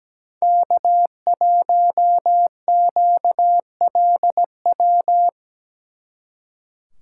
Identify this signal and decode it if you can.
Morse code, heard as a keyed tone, about 20 wpm, 700 Hz. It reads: K1QLW
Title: Speed Racing - USA 1x3 Call Signs 17wpm